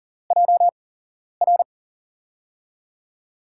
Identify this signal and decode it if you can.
Morse code, heard as a keyed tone, about 39 wpm, spaced as for 11 wpm, 700 Hz. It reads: JR
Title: USA Call Signs - 8x Character Spacing - 40wpm